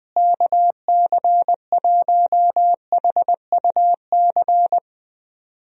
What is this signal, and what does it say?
Morse code, heard as a keyed tone, about 20 wpm, 700 Hz. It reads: KC1HUC
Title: USA Call Signs - 6 Characters - Repeated 2x - 20wpm